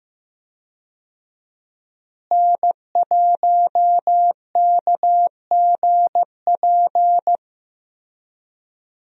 Morse code, heard as a keyed tone, about 16 wpm, 700 Hz. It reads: N1KGP
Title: USA Call Signs - 5 Characters - Repeated 3x - 15wpm